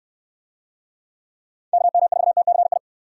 Morse code, heard as a keyed tone, about 69 wpm, 700 Hz. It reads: KM4TCI